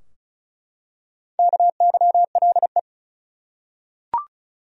Morse code, heard as a keyed tone, about 36 wpm, 700 Hz. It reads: KYLE
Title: USA Common Names - Max 5 Letters - 35wpm